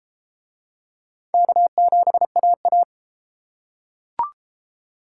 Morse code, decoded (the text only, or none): K7AA